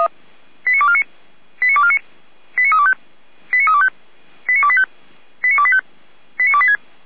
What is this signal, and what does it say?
Five-tone selective call (ZVEI), heard as quick runs of five tones, tones of about 70 ms. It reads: 89249 89249 89325 89325 89285 89285 89286
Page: /455/SDR/ZVEI/89340/